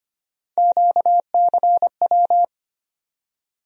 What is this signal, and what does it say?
Morse code, heard as a keyed tone, about 25 wpm, 700 Hz. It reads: QCW
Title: USA Call Sign Postfixes - 3 Letters 25wpm